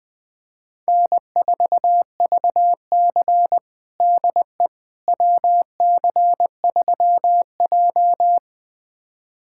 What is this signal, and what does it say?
Morse code, heard as a keyed tone, about 20 wpm, 700 Hz. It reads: N4VC DE WC3J
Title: USA Callsign DE Callsign - 4 Characters - 20wpm